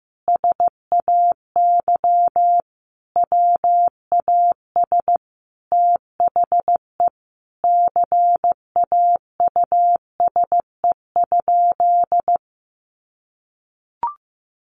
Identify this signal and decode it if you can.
Morse code, heard as a keyed tone, about 15 wpm, 700 Hz. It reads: SAY WAS THE CAUSE?
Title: Sentences from Top 200 Words 15wpm